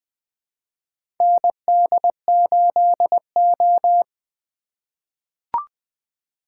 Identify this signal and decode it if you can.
Morse code, heard as a keyed tone, about 21 wpm, 700 Hz. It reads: ND8O
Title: USA Call Signs - 4 Characters - 2x Character Spacing - 20wpm